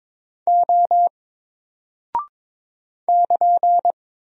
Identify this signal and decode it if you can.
Morse code, heard as a keyed tone, about 23 wpm, 700 Hz. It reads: O <KN>
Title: Single Character (Letter:Number:Punctuation:Prosign) Alt 22wpm